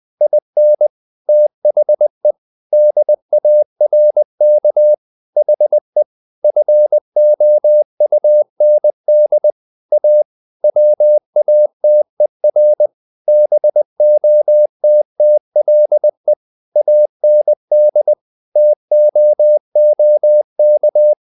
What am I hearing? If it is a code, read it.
Morse code, heard as a keyed tone, about 20 wpm, 600 Hz. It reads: IN THE DARK HE FOUND A WATER BOTTLE AND TOOK
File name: the old man and the sea part 2 20wpm